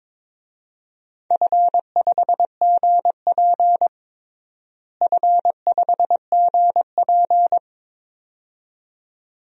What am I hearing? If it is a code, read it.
Morse code, heard as a keyed tone, about 23 wpm, 700 Hz. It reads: F5GP F5GP
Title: DXCC-100 Call Signs - 4 Characters - Repeated 2x - 22wpm